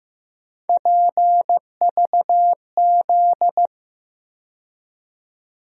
Morse code, heard as a keyed tone, about 15 wpm, 700 Hz. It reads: PVZ
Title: USA Call Sign Postfixes - 3 Letters 15wpm